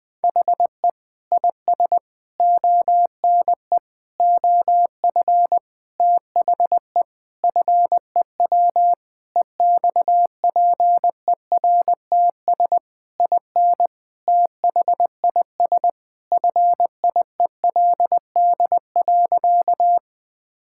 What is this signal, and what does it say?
Morse code, heard as a keyed tone, about 20 wpm, 700 Hz. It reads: HE IS ONE OF THE FEW EXPERTS IN THIS FIELD.
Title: Sentences with 4-Word Phrases - 20wpm